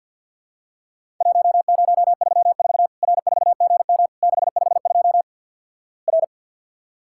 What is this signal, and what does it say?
Morse code, heard as a keyed tone, about 50 wpm, 700 Hz. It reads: 1034 R4GM 652 R
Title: Worked All Europe DX Contest - QTC Exchange with Repeats - 50wpm